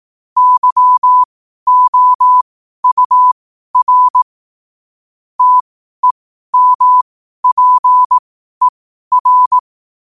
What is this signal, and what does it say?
Morse code, heard as a keyed tone, about 18 wpm, spaced as for 12 wpm, 1000 Hz. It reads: YOUR TEMPER
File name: qotdfiles / 12wpm